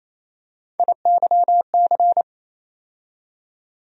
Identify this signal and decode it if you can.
Morse code, heard as a keyed tone, about 28 wpm, 700 Hz. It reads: IYC
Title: USA Call Sign Postfixes - 3 Letters 28wpm